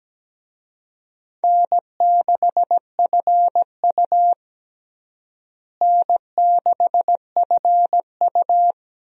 Morse code, heard as a keyed tone, about 17 wpm, 700 Hz. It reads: N6FU N6FU
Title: USA Call Signs - 4 Characters - Repeated 3x - 17wpm